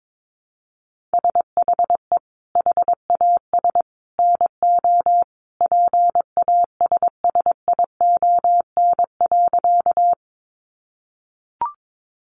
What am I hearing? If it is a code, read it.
Morse code, heard as a keyed tone, about 22 wpm, 700 Hz. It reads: SHE HAS NO PASSION.
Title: Sentences from Top 1300 Words 22wpm